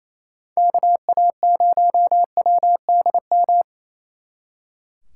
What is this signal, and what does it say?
Morse code, heard as a keyed tone, about 35 wpm, 700 Hz. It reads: KA0WDM